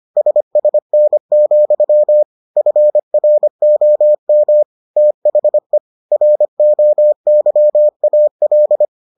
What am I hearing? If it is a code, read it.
Morse code, heard as a keyed tone, about 25 wpm, 600 Hz. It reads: SSN, FROM THE ROYAL